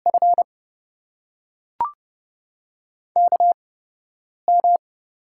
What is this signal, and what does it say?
Morse code, heard as a keyed tone, about 32 wpm, spaced as for 5 wpm, 700 Hz. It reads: FKM
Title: USA Call Signs - 5 Characters - 8x Character Spacing - 30wpm